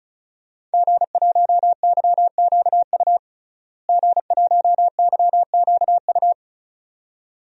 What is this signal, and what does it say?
Morse code, heard as a keyed tone, about 36 wpm, 700 Hz. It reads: G1YQU G1YQU
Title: DXCC-100 Call Signs - 5 Characters - Repeated 2x - 35wpm